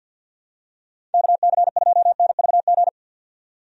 Morse code, heard as a keyed tone, about 49 wpm, 700 Hz. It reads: KC2NVG
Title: USA Call Signs - 6 Characters - 50wpm